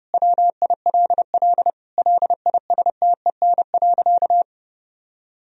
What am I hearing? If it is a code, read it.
Morse code, heard as a keyed tone, about 32 wpm, 700 Hz. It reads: WILL LISTEN.
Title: Sentences from Top 1000 Words - Review Part 3 - 30wpm